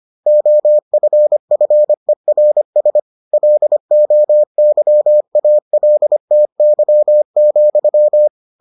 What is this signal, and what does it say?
Morse code, heard as a keyed tone, about 25 wpm, 600 Hz. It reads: OFFERS LOYALTY,